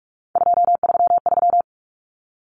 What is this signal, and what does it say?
Morse code, heard as a keyed tone, about 44 wpm, 700 Hz. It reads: 233
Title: Numbers - 3-Digit 45wpm